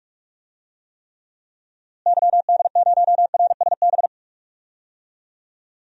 Morse code, heard as a keyed tone, about 46 wpm, 700 Hz. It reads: YD0RSB
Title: DXCC-100 Call Signs - 6 Characters - Repeated 3x - 45wpm